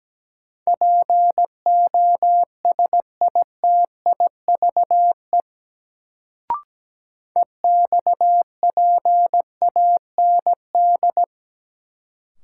Morse code, heard as a keyed tone, about 17 wpm, 700 Hz. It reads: POSITIVE EXPAND
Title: Sentences from Top 1400 Words - New Word Drill Alt - 17wpm